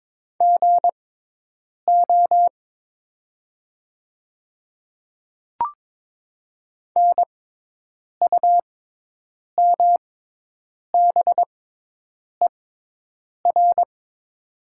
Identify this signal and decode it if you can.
Morse code, heard as a keyed tone, about 22 wpm, spaced as for 8 wpm, 700 Hz. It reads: GO NUMBER
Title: 6x Character Spacing - Top 100 Words - 22wpm